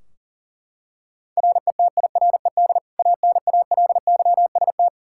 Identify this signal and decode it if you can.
Morse code, heard as a keyed tone, about 40 wpm, 700 Hz. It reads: RETIRED ANALYST
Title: K1USN SST QSO Practice 40wpm